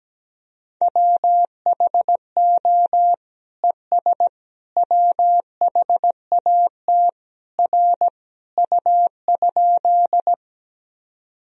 Morse code, heard as a keyed tone, about 17 wpm, 700 Hz. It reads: WHO ES WHAT R U?
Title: Sentences from Top 100 Words and Abbreviations 17wpm